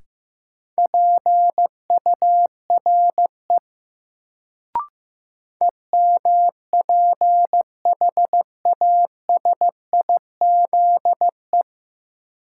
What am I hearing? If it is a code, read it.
Morse code, heard as a keyed tone, about 15 wpm, 700 Hz. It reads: PURE EMPHASIZE
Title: Sentences from Top 1500 Words - New Word Drill Alt - 15wpm